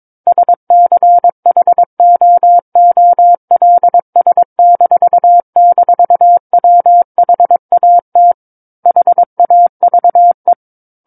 Morse code, heard as a keyed tone, about 22 wpm, 700 Hz. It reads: SCHOOLS--WHAT HAVE
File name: Book One / 13-Chapter 13 - 22 WPM